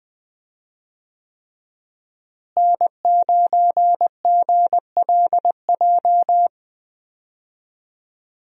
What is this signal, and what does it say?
Morse code, heard as a keyed tone, about 20 wpm, 700 Hz. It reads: N9GLJ
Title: USA Call Signs - 5 Characters - Repeated 3x - 20wpm